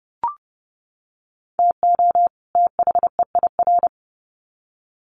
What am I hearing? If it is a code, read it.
Morse code, heard as a keyed tone, about 30 wpm, 700 Hz. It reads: TO THEIR
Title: Sets of 2 Words 30wpm